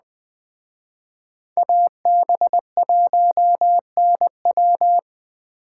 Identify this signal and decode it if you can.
Morse code, heard as a keyed tone, about 20 wpm, 700 Hz. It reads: AB1NW
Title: Speed Racing - USA 2x2 Call Signs 17wpm